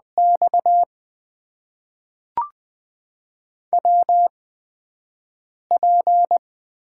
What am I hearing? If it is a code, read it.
Morse code, heard as a keyed tone, about 20 wpm, spaced as for 5 wpm, 700 Hz. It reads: X WP
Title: USA Call Signs - 8x Character Spacing - 20wpm